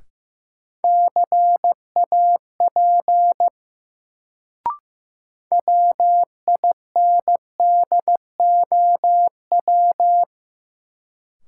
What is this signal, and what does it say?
Morse code, heard as a keyed tone, about 15 wpm, 700 Hz. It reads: CAP WINDOW
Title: Sentences from Top 1600 Words - New Word Drill - 15wpm